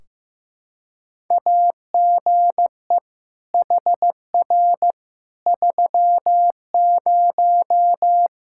Morse code, heard as a keyed tone, about 15 wpm, 700 Hz. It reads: AGE HR 30
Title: Practice QSO - Exchange 3 - More Info - 15wpm